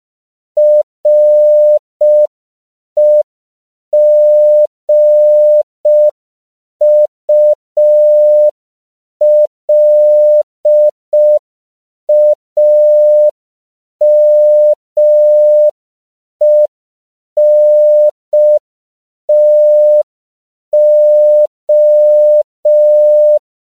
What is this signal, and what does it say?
Morse code, heard as a keyed tone, about 5 wpm, 600 Hz. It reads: REGULAMENTO
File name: CW / 5PPM